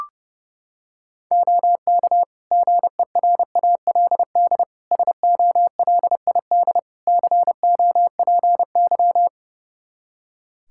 Morse code, heard as a keyed tone, about 30 wpm, 700 Hz. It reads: OK GERALD SOLID COPY
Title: Practice QSO - Exchange 3 - More Info - 30wpm